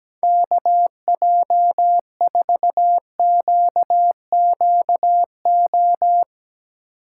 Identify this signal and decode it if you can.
Morse code, heard as a keyed tone, about 17 wpm, 700 Hz. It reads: KJ4QQO
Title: USA Call Signs - 6 Characters - 6x Character Spacing - 17wpm